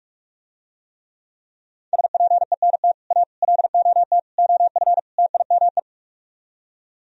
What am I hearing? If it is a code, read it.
Morse code, heard as a keyed tone, about 46 wpm, 700 Hz. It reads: SPENT A LOT OF TIME